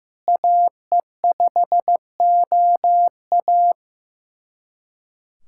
Morse code, heard as a keyed tone, about 15 wpm, 700 Hz. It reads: AE5OA
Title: USA Callsign DE Callsign 15wpm